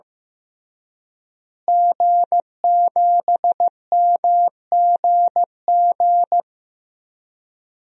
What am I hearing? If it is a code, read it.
Morse code, heard as a keyed tone, about 16 wpm, 700 Hz. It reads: G7MGG
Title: DXCC-100 Call Signs - 5 Characters - Repeated 2x - 15wpm